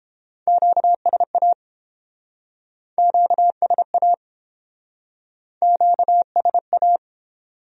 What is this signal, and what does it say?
Morse code, heard as a keyed tone, about 29 wpm, 700 Hz. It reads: QSA QSA QSA